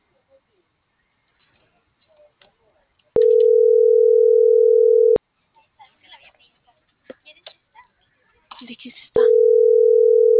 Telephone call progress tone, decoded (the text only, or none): ringback tone